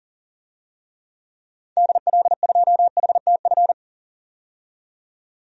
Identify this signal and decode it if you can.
Morse code, heard as a keyed tone, about 40 wpm, 700 Hz. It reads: DL2HTF
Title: DXCC-100 Call Signs - 6 Characters - 40wpm